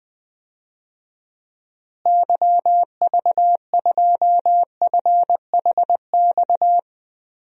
Morse code, heard as a keyed tone, about 20 wpm, 700 Hz. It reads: YV2FHX
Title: DXCC-100 Call Signs - 6 Characters - Repeated 2x - 20wpm